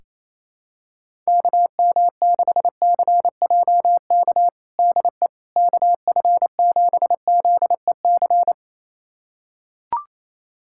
Morse code, heard as a keyed tone, about 28 wpm, 700 Hz. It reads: KM6CJK DE KF7ZEC
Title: USA Callsign DE Callsign - 6 Characters - 28wpm